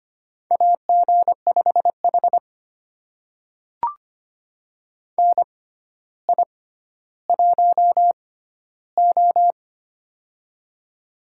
Morse code, heard as a keyed tone, about 25 wpm, spaced as for 13 wpm, 700 Hz. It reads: AG5H NI1O
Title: USA Call Signs - 4 Characters - 6x Character Spacing - 25wpm